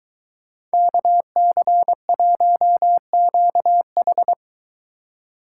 Morse code, heard as a keyed tone, about 21 wpm, 700 Hz. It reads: KC1QH